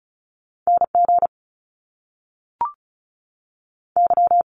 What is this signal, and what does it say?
Morse code, heard as a keyed tone, about 36 wpm, 700 Hz. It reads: N G Y
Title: Top Two Letter Combinations in Words 35wpm